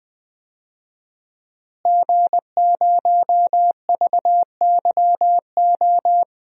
Morse code, heard as a keyed tone, about 21 wpm, 700 Hz. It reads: G0VYO
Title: DXCC-100 Call Signs - 5 Characters - Repeated 3x - 20wpm